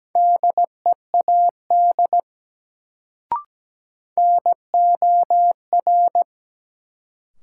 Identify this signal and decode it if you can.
Morse code, heard as a keyed tone, about 17 wpm, 700 Hz. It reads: DEAD NOR
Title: Sentences from Top 1500 Words - New Word Drill Alt - 17wpm